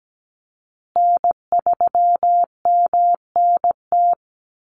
Morse code, heard as a keyed tone, about 17 wpm, 700 Hz. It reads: N3MNT
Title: Call Signs Repeated 2x - 17wpm